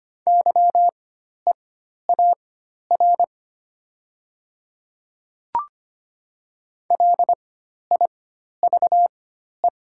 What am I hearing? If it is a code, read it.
Morse code, heard as a keyed tone, about 25 wpm, spaced as for 12 wpm, 700 Hz. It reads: YEAR LIVE
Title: 4x Character Spacing - Top 200 Words - 25wpm